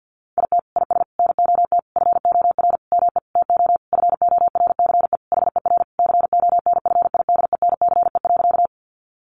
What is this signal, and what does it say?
Morse code, heard as a keyed tone, about 50 wpm, 700 Hz. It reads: IT IS NOT FOR ME TO FORCE HER CONFIDENCE.
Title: Sentences from Top 1500 Words 50wpm